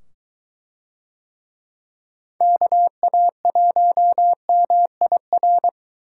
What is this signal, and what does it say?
Morse code, heard as a keyed tone, about 23 wpm, 700 Hz. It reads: KA1MIR